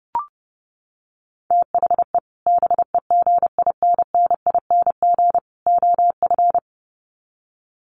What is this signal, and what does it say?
Morse code, heard as a keyed tone, about 30 wpm, 700 Hz. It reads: THE BEGINNING OF
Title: Sets of 3 Words 30wpm